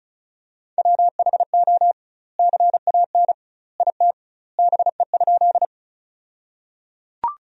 Morse code, heard as a keyed tone, about 36 wpm, 700 Hz. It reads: WHO CAN IT BE?
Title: 2x Word-Spacing - Sentences from Top 100 Words - 35wpm